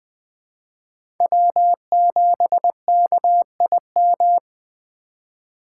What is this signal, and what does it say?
Morse code, heard as a keyed tone, about 20 wpm, 700 Hz. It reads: W7KIM
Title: USA Call Signs - 5 Characters - 3x Character Spacing - 20wpm